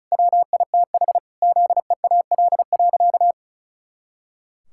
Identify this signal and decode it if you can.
Morse code, heard as a keyed tone, about 36 wpm, 700 Hz. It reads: WITH ZEAL.
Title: Pangram Challenge v2 - 35wpm